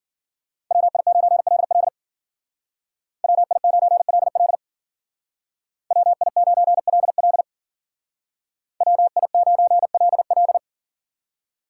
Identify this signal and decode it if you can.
Morse code, heard as a keyed tone, about 52 wpm, 700 Hz. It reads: WI9LL WI9LL WI9LL WI9LL